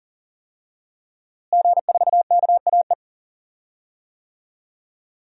Morse code, heard as a keyed tone, about 41 wpm, 700 Hz. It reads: G4KAE
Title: DXCC-100 Call Signs - 5 Characters - Repeated 2x - 40wpm